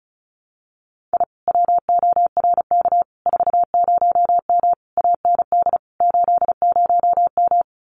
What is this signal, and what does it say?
Morse code, heard as a keyed tone, about 36 wpm, 700 Hz. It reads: I WORK 40M AND 80M
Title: Common QSO Phrases Encore 35wpm